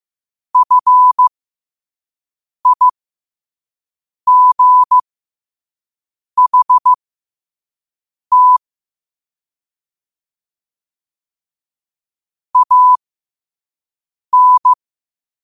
Morse code, heard as a keyed tone, about 15 wpm, spaced as for 5 wpm, 1000 Hz. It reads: FIGHT AN
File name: qotdfiles / 5wpm